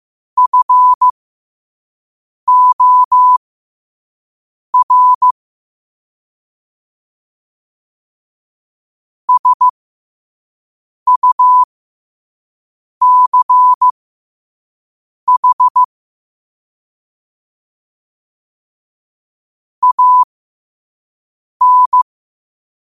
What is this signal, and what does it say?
Morse code, heard as a keyed tone, about 15 wpm, spaced as for 5 wpm, 1000 Hz. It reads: FOR SUCH AN
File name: qotdfiles / 5wpm